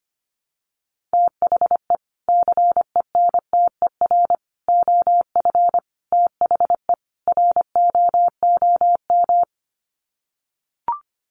Morse code, heard as a keyed tone, about 25 wpm, 700 Hz. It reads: THE CENTER OF THE ROOM
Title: Sets of 5 Words Encore 25wpm